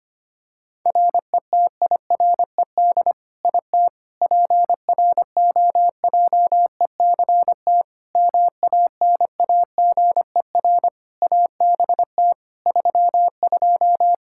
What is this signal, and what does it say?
Morse code, heard as a keyed tone, about 25 wpm, 700 Hz. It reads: RETIRED IT PROJECT MANAGER ABT 32
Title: Practice QSO - Exchange 3 - More Info - 25wpm